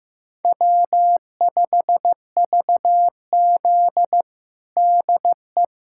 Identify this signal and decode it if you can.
Morse code, heard as a keyed tone, about 15 wpm, 700 Hz. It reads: W5VZ DE
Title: USA Callsign DE Callsign - 4 Characters - 15wpm